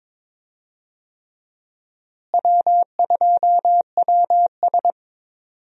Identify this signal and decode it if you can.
Morse code, heard as a keyed tone, about 22 wpm, 700 Hz. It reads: W2WS
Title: USA Call Signs - 4 Characters - Repeated 3x - 22wpm